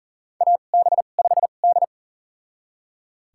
Morse code, heard as a keyed tone, about 40 wpm, spaced as for 29 wpm, 700 Hz. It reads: AB5D
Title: USA Call Signs - 4 Characters - 2x Character Spacing - 40wpm